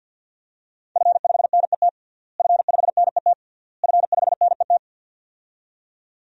Morse code, heard as a keyed tone, about 48 wpm, 700 Hz. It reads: F5NET F5NET F5NET
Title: DXCC-100 Call Signs - 5 Characters - Repeated 3x - 50wpm